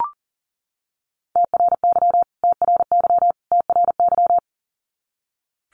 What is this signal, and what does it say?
Morse code, heard as a keyed tone, about 40 wpm, 700 Hz. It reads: TRY TRY TRY
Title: Top 200 Words Repeated 40wpm